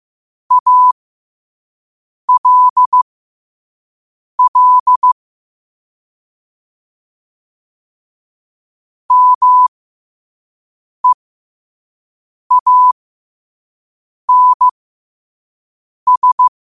Morse code, heard as a keyed tone, about 15 wpm, spaced as for 5 wpm, 1000 Hz. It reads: ALL MEANS
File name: qotdfiles / 5wpm